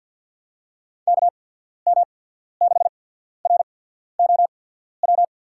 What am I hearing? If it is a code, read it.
Morse code, heard as a keyed tone, about 50 wpm, spaced as for 14 wpm, 700 Hz. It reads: KM6ROW
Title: USA Call Signs - 6 Characters - 8x Character Spacing - 50wpm